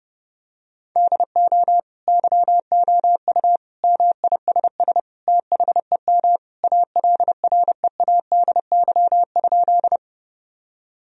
Morse code, heard as a keyed tone, about 32 wpm, 700 Hz. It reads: DO YOU MISS THEM ALREADY?
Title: Sentences from Top 1000 Words - Review Part 3 - 30wpm